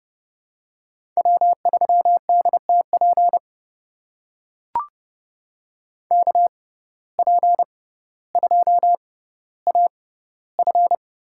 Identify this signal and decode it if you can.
Morse code, heard as a keyed tone, about 30 wpm, spaced as for 15 wpm, 700 Hz. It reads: W3DTP KP2AF
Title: USA Call Signs - 5 Characters - 6x Character Spacing - 30wpm